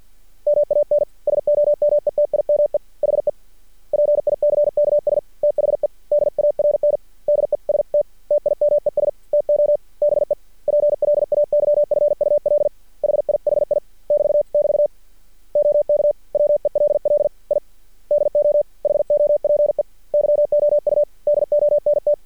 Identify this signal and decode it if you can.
Morse code, heard as a keyed tone, about 50 wpm, 590 Hz. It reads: MAN SOMETIME HE PICKS THE DARNDEST TIMES TO BE PLAYFUL HIHI <BT><BT> OK WELL I DO HOPE YOU DONT
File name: archived / CRSnet